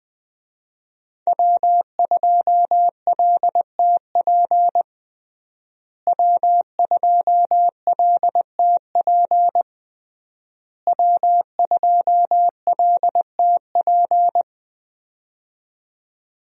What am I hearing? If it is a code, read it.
Morse code, heard as a keyed tone, about 20 wpm, 700 Hz. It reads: W2LTP W2LTP W2LTP
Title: USA Call Signs - 5 Characters - Repeated 3x - 20wpm